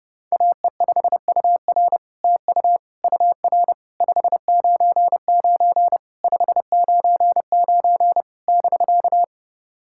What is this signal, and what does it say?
Morse code, heard as a keyed tone, about 30 wpm, 700 Hz. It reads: AE5UR TU UR 599 599 <BK>
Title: SOTA QSO Practice 30wpm